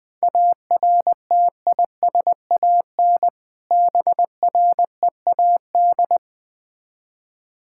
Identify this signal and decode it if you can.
Morse code, heard as a keyed tone, about 20 wpm, 690 Hz. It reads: ARTISAN BREAD
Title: Difficult Words with Hints - 20wpm